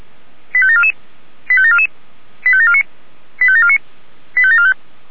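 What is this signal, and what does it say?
Five-tone selective call (ZVEI), heard as quick runs of five tones, tones of about 70 ms. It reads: 85744 85744 85749 85749 85745
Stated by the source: Received by SDR, ZVEI five-tone call 85726.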